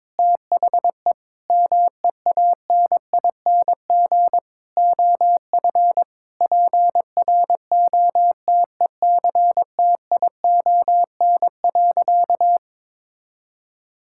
Morse code, heard as a keyed tone, about 22 wpm, 700 Hz. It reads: THE MEANING OF PROTECTION.
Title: Sentences from Top 1600 Words 22wpm